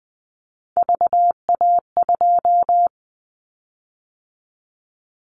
Morse code, heard as a keyed tone, about 20 wpm, 700 Hz. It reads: VA2
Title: Callsign Prefixes for Top 25 DXCC - 20wpm